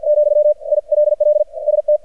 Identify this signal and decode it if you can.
Morse code, heard as a keyed tone, about 48 wpm, 600 Hz, with radio noise. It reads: , ILL ST
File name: archived / CRSnet